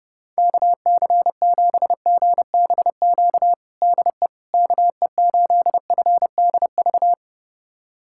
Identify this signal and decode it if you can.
Morse code, heard as a keyed tone, about 32 wpm, 700 Hz. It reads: KC7GBQ DE KE8FDV
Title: USA Callsign DE Callsign - 6 Characters - 30wpm